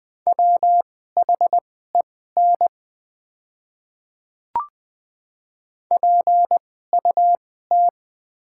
Morse code, heard as a keyed tone, about 21 wpm, spaced as for 15 wpm, 700 Hz. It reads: WHEN PUT
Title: 2x Character Spacing - Top 200 Words - 20wpm